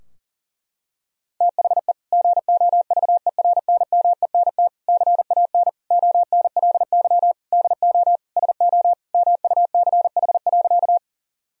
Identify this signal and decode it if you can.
Morse code, heard as a keyed tone, about 40 wpm, 700 Hz. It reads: THE GOVERNMENT CAN ONLY DO SO MUCH.
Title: Sentences from Top 1000 Words - Review Part 3 - 40wpm